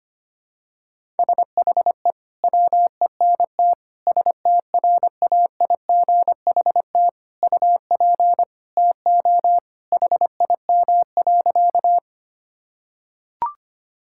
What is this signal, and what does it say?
Morse code, heard as a keyed tone, about 25 wpm, 700 Hz. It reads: SHE WENT STRAIGHT UP TO HIM.
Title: Sentences from Top 1000 Words - Review Part 6 - 25wpm